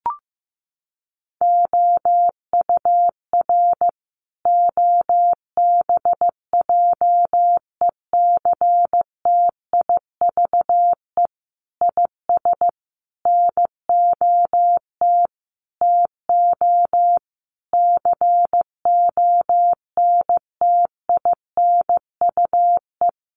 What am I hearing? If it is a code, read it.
Morse code, heard as a keyed tone, about 15 wpm, 700 Hz. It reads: OUR OBJECTIVE IS NOT TO CONTINUE
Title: Sentences from Top 1200 Words 15wpm